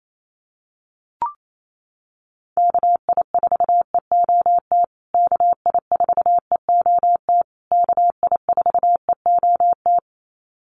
Morse code, heard as a keyed tone, about 28 wpm, 700 Hz. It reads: KI4EOT KI4EOT KI4EOT
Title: USA Call Signs Repeated 3x 28wpm